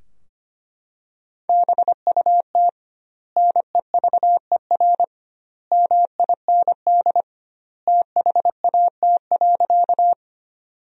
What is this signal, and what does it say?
Morse code, heard as a keyed tone, about 25 wpm, 700 Hz. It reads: BUT NEVER MIND THAT.
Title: 2x Word-Spacing - Sentences from Top 500 Words - 25wpm